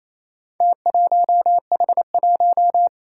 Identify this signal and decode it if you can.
Morse code, heard as a keyed tone, about 28 wpm, 700 Hz. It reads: T1H1
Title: Four Challenging Letter-Number Combinations 28wpm